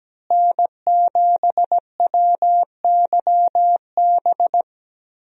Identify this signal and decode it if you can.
Morse code, heard as a keyed tone, about 17 wpm, 700 Hz. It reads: N7WYB
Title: USA Call Signs - 5 Characters - Repeated 3x - 17wpm